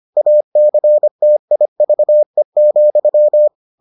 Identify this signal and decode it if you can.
Morse code, heard as a keyed tone, about 25 wpm, 600 Hz. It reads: ACTIVE,